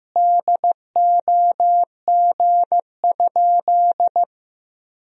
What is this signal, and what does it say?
Morse code, heard as a keyed tone, about 16 wpm, 700 Hz. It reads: DOG?
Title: Sentences from Top 1000 Words - Review Part 2 - 15wpm